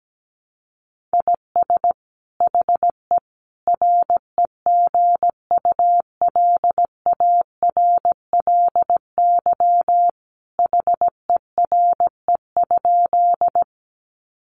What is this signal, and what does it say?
Morse code, heard as a keyed tone, about 17 wpm, 700 Hz. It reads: IS HE REGULARLY HERE?
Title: Sentences from Top 1400 Words 17wpm